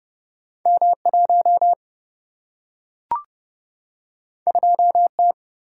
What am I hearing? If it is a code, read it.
Morse code, heard as a keyed tone, about 30 wpm, 700 Hz. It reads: M1 2T
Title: Single-Letter Single-Number 30wpm